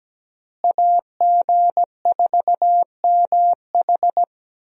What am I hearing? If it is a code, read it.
Morse code, heard as a keyed tone, about 17 wpm, 700 Hz. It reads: AG4MH
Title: USA Call Signs - 5 Characters - Repeated 3x - 17wpm